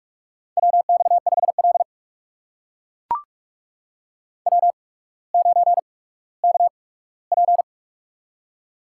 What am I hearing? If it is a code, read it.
Morse code, heard as a keyed tone, about 44 wpm, spaced as for 19 wpm, 700 Hz. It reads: WX5L W9KP
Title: USA Call Signs - 4 Characters - 8x Character Spacing - 45wpm